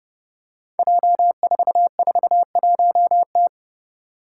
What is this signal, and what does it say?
Morse code, heard as a keyed tone, about 32 wpm, 700 Hz. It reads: J441T
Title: Five Challenging Letter-Number Combinations 30wpm